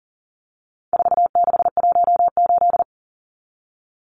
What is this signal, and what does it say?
Morse code, heard as a keyed tone, about 40 wpm, 700 Hz. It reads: 4618
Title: Numbers - 4-Digit 40wpm